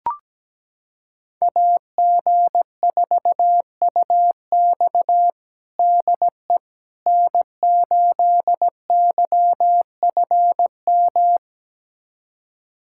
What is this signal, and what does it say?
Morse code, heard as a keyed tone, about 17 wpm, 700 Hz. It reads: AG4UX DE N8YFM
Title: USA Callsign DE Callsign - 5 Characters - 17wpm